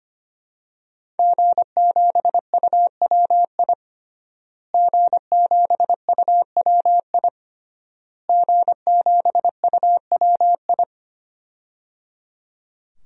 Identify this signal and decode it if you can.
Morse code, heard as a keyed tone, about 25 wpm, 700 Hz. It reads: G7UWI G7UWI G7UWI
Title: DXCC-100 Call Signs - 5 Characters - Repeated 3x - 25wpm